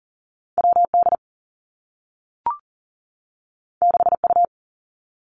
Morse code, heard as a keyed tone, about 38 wpm, 700 Hz. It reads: W D 6 U
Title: Two Challenging Letter-Number Combinations 40wpm